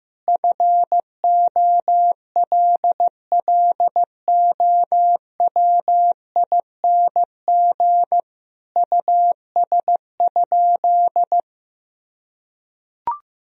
Morse code, heard as a keyed tone, about 15 wpm, 700 Hz. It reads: FOLLOWING US?
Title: Sentences from Top 1000 Words - Review Part 4 - 15wpm